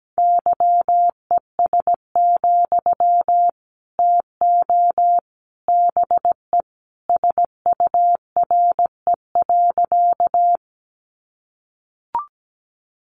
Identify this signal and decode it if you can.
Morse code, heard as a keyed tone, about 17 wpm, 700 Hz. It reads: YES, TO BE SURE.
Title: Sentences from Top 500 Words 17wpm